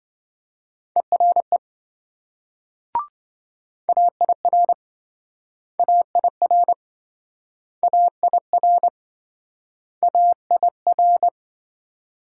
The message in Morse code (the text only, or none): ERE AIR AIR AIR AIR